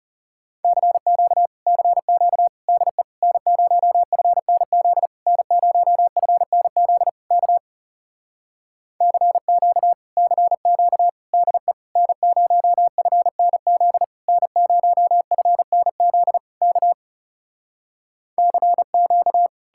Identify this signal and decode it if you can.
Morse code, heard as a keyed tone, about 40 wpm, 700 Hz. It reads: CQ CQ DE N0FNZ N0FNZ K CQ CQ DE N0FNZ N0FNZ K CQ